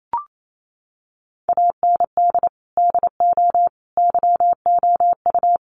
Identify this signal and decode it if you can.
Morse code, heard as a keyed tone, about 28 wpm, 700 Hz. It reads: AND DO YOU
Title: Sentences from Top 600 Words 28wpm